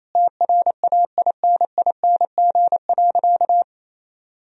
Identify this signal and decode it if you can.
Morse code, heard as a keyed tone, about 28 wpm, 700 Hz. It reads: TRAINING.
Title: Sentences from Top 1000 Words - Review Part 4 - 28wpm